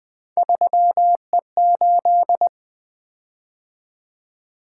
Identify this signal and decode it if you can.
Morse code, heard as a keyed tone, about 20 wpm, 700 Hz. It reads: 3E8
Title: Three Letter-Number Combinations - 20wpm